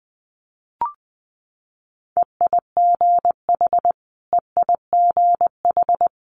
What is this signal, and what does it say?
Morse code, heard as a keyed tone, about 20 wpm, 700 Hz. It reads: EIGH EIGH
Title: Multi-letter Phonemes Repeated 3x 20wpm